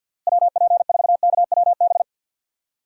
Morse code, heard as a keyed tone, about 48 wpm, 700 Hz. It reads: WP4KWB
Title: USA Call Signs - 6 Characters - Repeated 3x - 50wpm